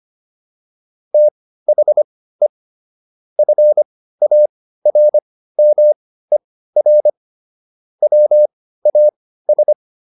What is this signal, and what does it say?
Morse code, heard as a keyed tone, about 25 wpm, spaced as for 15 wpm, 600 Hz. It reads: THE FARMER WAS